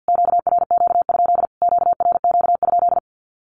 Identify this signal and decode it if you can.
Morse code, heard as a keyed tone, about 50 wpm, 700 Hz. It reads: QRQ? QRQ?
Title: Q-Codes Repeated 3x 50wpm